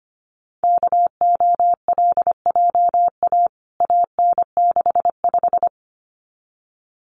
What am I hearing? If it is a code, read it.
Morse code, heard as a keyed tone, about 25 wpm, 700 Hz. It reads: KOLJA AN65
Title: CWT Exchange 25wpm